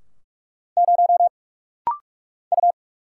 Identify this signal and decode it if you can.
Morse code, heard as a keyed tone, about 44 wpm, 700 Hz. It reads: 0 U
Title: Single Letter-Number Alt - Rapid-Fire 45wpm